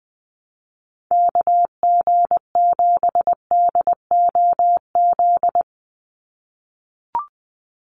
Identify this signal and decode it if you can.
Morse code, heard as a keyed tone, about 20 wpm, 700 Hz. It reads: KG7DOZ
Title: USA Call Signs 20wpm